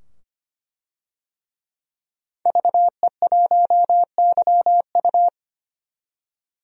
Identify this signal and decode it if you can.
Morse code, heard as a keyed tone, about 25 wpm, 690 Hz. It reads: VE1YU
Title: DXCC-100 Call Signs - 5 Characters - 25wpm